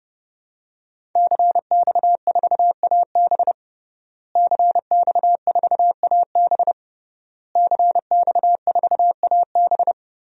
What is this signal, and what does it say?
Morse code, heard as a keyed tone, about 32 wpm, 700 Hz. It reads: CX4AB CX4AB CX4AB
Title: DXCC-100 Call Signs - 5 Characters - Repeated 3x - 30wpm